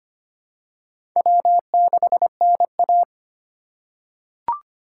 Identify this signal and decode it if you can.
Morse code, heard as a keyed tone, about 25 wpm, 700 Hz. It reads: W6NA
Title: USA Call Signs - 4 Characters - 2x Character Spacing - 25wpm